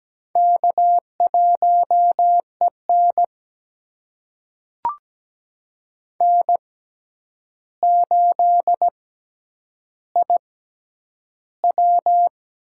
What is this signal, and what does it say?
Morse code, heard as a keyed tone, about 17 wpm, spaced as for 9 wpm, 700 Hz. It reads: K1EN N8IW
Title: USA Call Signs - 4 Characters - 6x Character Spacing - 17wpm